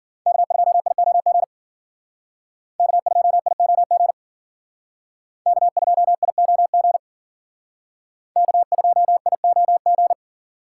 Morse code, heard as a keyed tone, about 51 wpm, 700 Hz. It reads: K2IOG K2IOG K2IOG K2IOG